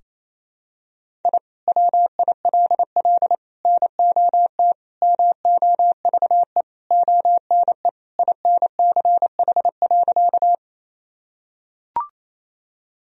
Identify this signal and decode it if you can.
Morse code, heard as a keyed tone, about 28 wpm, 700 Hz. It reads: I WILL NOT MOVE ONE INCH.
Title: Sentences from Top 1000 Words - Review Part 3 - 28wpm